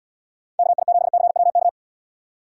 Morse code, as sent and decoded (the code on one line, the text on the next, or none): -.- .. --... -.-. .-- -.-
KI7CWK